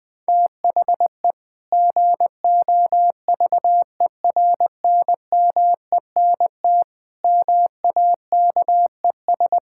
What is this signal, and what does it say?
Morse code, heard as a keyed tone, about 21 wpm, 700 Hz. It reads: THE GOVERNMENT MAKES
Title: Sentences with 2-Word Phrases - Spoken Beginning - 20wpm